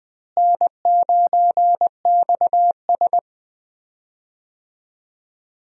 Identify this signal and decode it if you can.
Morse code, heard as a keyed tone, about 20 wpm, 700 Hz. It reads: N9XS
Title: USA Call Signs - 4 Characters - Repeated 3x - 20wpm